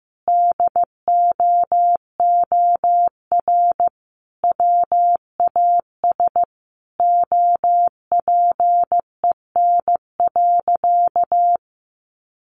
Morse code, heard as a keyed tone, about 15 wpm, 700 Hz. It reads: DOOR WAS OPEN.
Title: Sentences from Top 1400 Words 15wpm